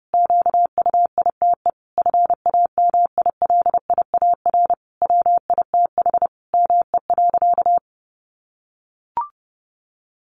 Morse code, heard as a keyed tone, about 30 wpm, 700 Hz. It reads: QUITE FAMILIAR WITH ME.
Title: Sentences from Top 1000 Words 30wpm